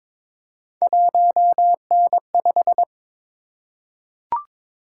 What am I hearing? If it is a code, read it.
Morse code, heard as a keyed tone, about 22 wpm, 700 Hz. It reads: 1N5
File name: practice / 22wpm / Three Letter-Number Combinations - 22wpm